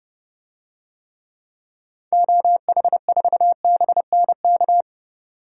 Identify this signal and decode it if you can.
Morse code, heard as a keyed tone, about 32 wpm, 700 Hz. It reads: OH4BNK
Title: DXCC-100 Call Signs - 6 Characters - 30wpm